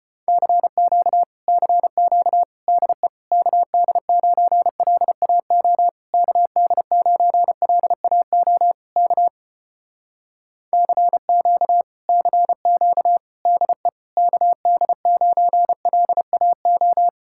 Morse code, heard as a keyed tone, about 33 wpm, 700 Hz. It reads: CQ CQ DE KD9LAO KD9LAO K CQ CQ DE KD9LAO